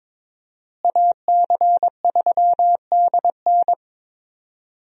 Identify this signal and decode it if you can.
Morse code, heard as a keyed tone, about 22 wpm, 700 Hz. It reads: AC3DN
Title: USA Call Signs - 5 Characters - Repeated 3x - 22wpm